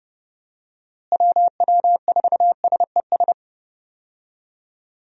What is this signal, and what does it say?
Morse code, heard as a keyed tone, about 31 wpm, 700 Hz. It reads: WW4SES
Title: USA Call Signs - 6 Characters - Repeated 2x - 30wpm